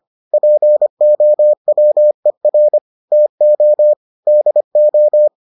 Morse code, heard as a keyed tone, about 25 wpm, 600 Hz. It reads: POWER TO DO